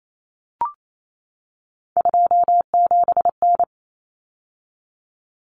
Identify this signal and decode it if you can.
Morse code, heard as a keyed tone, about 28 wpm, 700 Hz. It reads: 27N
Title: Common Cut Numbers - 3-Digit 28wpm